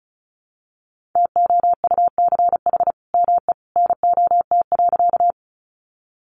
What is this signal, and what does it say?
Morse code, heard as a keyed tone, about 34 wpm, 700 Hz. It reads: TOUCH ME NOT.
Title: Sentences from Top 1200 Words 35wpm